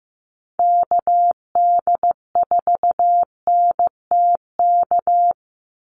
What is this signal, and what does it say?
Morse code, heard as a keyed tone, about 15 wpm, 700 Hz. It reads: KD4NTK
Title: USA Call Signs 15wpm